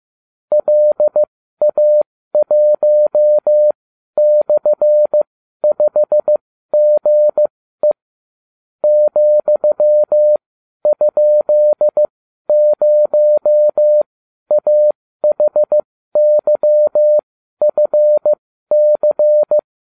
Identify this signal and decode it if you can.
Morse code, heard as a keyed tone, about 15 wpm, spaced as for 12 wpm, 600 Hz. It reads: LA1/5GE ,?0AHYFC